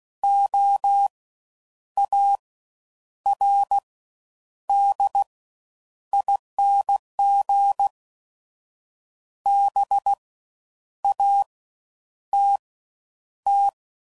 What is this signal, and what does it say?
Morse code, heard as a keyed tone, about 16 wpm, 800 Hz. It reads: O A R D ING B A T T
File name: Suffixes / Special INGs / Fast Ings